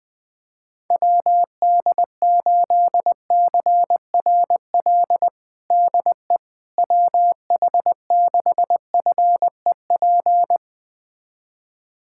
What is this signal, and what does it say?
Morse code, heard as a keyed tone, about 20 wpm, 700 Hz. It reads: WD8CRL DE WH6FEP
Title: USA Callsign DE Callsign - 6 Characters - 20wpm